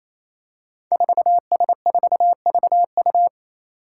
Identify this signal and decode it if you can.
Morse code, heard as a keyed tone, about 28 wpm, 700 Hz. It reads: 4S4VU
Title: Five Challenging Letter-Number Combinations 28wpm